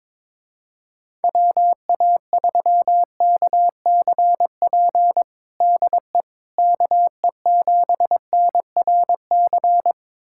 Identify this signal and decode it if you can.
Morse code, heard as a keyed tone, about 22 wpm, 700 Hz. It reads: WA3KCP DE KE7NRC
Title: USA Callsign DE Callsign - 6 Characters - 22wpm